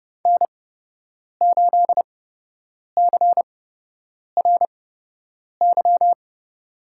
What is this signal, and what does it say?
Morse code, heard as a keyed tone, about 30 wpm, spaced as for 8 wpm, 700 Hz. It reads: N8CRY
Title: USA Call Signs - 5 Characters - 8x Character Spacing - 30wpm